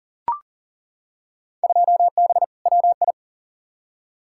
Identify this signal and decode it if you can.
Morse code, heard as a keyed tone, about 40 wpm, 700 Hz. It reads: 2B WI
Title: Field Day Exchanges - 40wpm